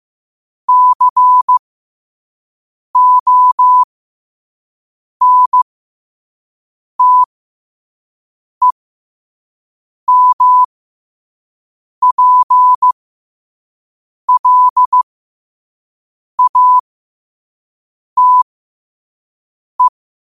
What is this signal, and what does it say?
Morse code, heard as a keyed tone, about 15 wpm, spaced as for 5 wpm, 1000 Hz. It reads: CONTEMPLATE
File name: qotdfiles / 5wpm